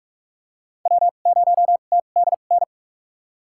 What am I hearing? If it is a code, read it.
Morse code, heard as a keyed tone, about 47 wpm, spaced as for 33 wpm, 700 Hz. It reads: W0TDN
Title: USA Call Signs - 5 Characters - 2x Character Spacing - 45wpm